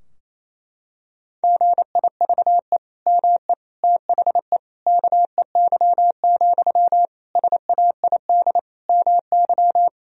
Morse code, heard as a keyed tone, about 28 wpm, 700 Hz. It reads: GIVE ME THE KEY, SAID MY
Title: Sentences from Top 1000 Words - Review Part 4 - 28wpm